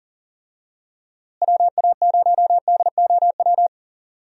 Morse code, heard as a keyed tone, about 40 wpm, 700 Hz. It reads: WA0DOW
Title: USA Call Signs - 6 Characters - Repeated 2x - 40wpm